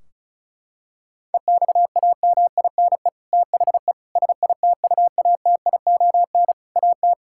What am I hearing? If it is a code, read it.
Morse code, heard as a keyed tone, about 36 wpm, 700 Hz. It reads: EXAMINE THE SITUATION AT
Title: Sentences from Top 1600 Words 35wpm